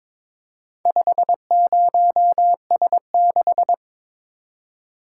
Morse code, heard as a keyed tone, about 22 wpm, 700 Hz. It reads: 50S6
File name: practice / 22wpm / Four Letter-Number Combinations - 22wpm